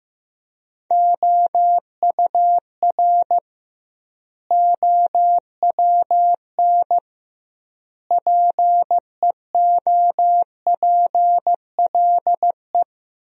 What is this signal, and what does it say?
Morse code, heard as a keyed tone, about 15 wpm, 700 Hz. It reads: OUR OWN PEOPLE
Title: 2x Word-Spacing - Sentences from Top 300 Words - 15wpm